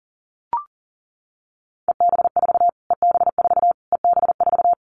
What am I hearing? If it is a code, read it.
Morse code, heard as a keyed tone, about 40 wpm, 700 Hz. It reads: EB4 EB4 EB4
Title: Callsign Prefixes for Top 25 DXCC - Repeated 3x - 40wpm